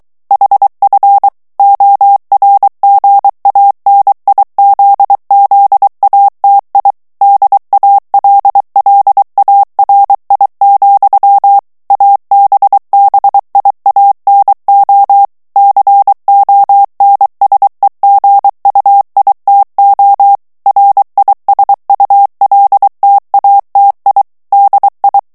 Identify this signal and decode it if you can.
Morse code, heard as a keyed tone, about 23 wpm, 800 Hz. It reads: HF ORGANIZZATI DALLARI, ABBIANO CONSEGUITO RISULTATI DI
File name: ITALIEN texte clair 2 vitesse 22 mots minute